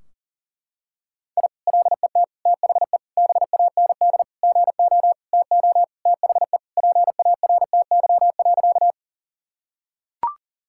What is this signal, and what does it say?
Morse code, heard as a keyed tone, about 40 wpm, 700 Hz. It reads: I LET THE BAND GO TO THE PARTY.
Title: Sentences from Top 1600 Words 40wpm